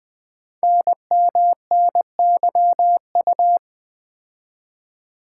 Morse code, heard as a keyed tone, about 20 wpm, 700 Hz. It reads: NMNYU
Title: Code Groups - 5 Letters - 20wpm